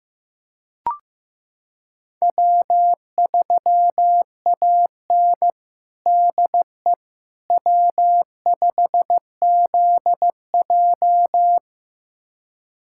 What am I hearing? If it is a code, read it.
Morse code, heard as a keyed tone, about 16 wpm, 700 Hz. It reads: W3AN DE W5ZJ
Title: USA Callsign DE Callsign - 4 Characters - 15wpm